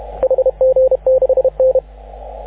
Morse code, heard as a keyed tone, about 30 wpm, 520 Hz. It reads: HG6N
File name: AUDIO2019 / HG6N CW SSB MIX